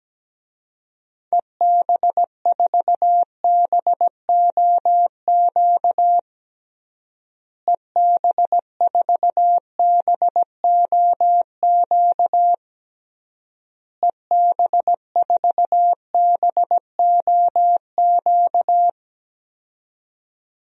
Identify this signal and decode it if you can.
Morse code, heard as a keyed tone, about 17 wpm, 700 Hz. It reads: EB4BOQ EB4BOQ EB4BOQ
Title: DXCC-100 Call Signs - 6 Characters - Repeated 3x - 17wpm